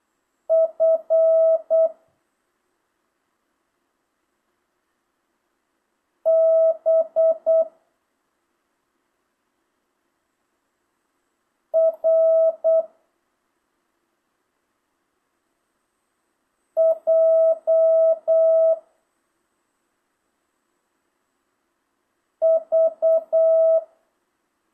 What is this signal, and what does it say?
Morse code, heard as a keyed tone, about 8 wpm, 650 Hz. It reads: F B R J V
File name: jh8cbh / 2025cw